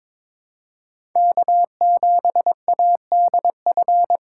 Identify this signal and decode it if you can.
Morse code, heard as a keyed tone, about 22 wpm, 700 Hz. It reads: K7ADF
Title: USA Call Signs - 5 Characters - 3x Character Spacing - 22wpm